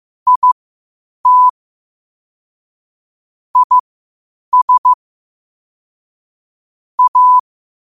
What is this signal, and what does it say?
Morse code, heard as a keyed tone, about 15 wpm, spaced as for 8 wpm, 1000 Hz. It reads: IT IS A
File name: qotdfiles / 8wpm